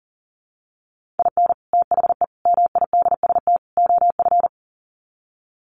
Morse code, heard as a keyed tone, about 40 wpm, 700 Hz. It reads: IN THE MIDST OF
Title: Sets of 4 Words 40wpm